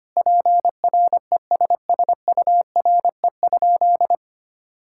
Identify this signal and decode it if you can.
Morse code, heard as a keyed tone, about 25 wpm, 700 Hz. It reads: PRESSURE?
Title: Sentences from Top 1000 Words - Review Part 6 - 25wpm